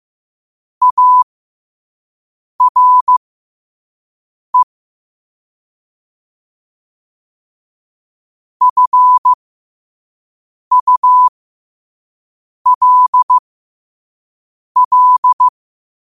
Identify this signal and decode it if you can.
Morse code, heard as a keyed tone, about 15 wpm, spaced as for 5 wpm, 1000 Hz. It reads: ARE FULL
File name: qotdfiles / 5wpm